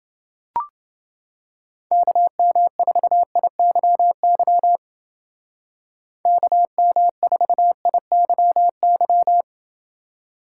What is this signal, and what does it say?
Morse code, heard as a keyed tone, about 29 wpm, 700 Hz. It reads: KM4IYY KM4IYY